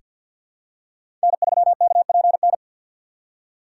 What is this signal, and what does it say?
Morse code, heard as a keyed tone, about 49 wpm, 700 Hz. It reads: N3KPN
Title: USA Call Signs - 5 Characters - 2x Character Spacing - 50wpm